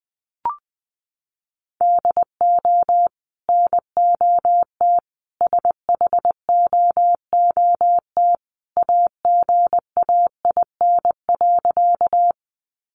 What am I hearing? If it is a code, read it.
Morse code, heard as a keyed tone, about 20 wpm, 700 Hz. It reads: DO NOT SHOOT AGAIN.
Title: Sentences from Top 1200 Words 20wpm